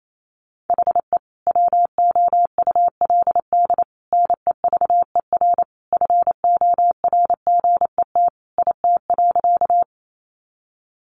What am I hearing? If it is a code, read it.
Morse code, heard as a keyed tone, about 28 wpm, 700 Hz. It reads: HE WOULD NEVER FORGET IT.
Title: Sentences from Top 1200 Words 28wpm